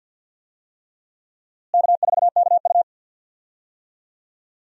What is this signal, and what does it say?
Morse code, heard as a keyed tone, about 53 wpm, 700 Hz. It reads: K4KU
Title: USA Call Signs - 4 Characters - Repeated 3x - 50wpm